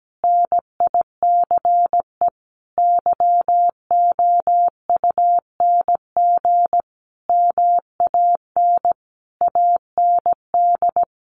Morse code, heard as a keyed tone, about 17 wpm, 700 Hz. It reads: NICE YOUNG MAN AND
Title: Sentences from Top 1200 Words 17wpm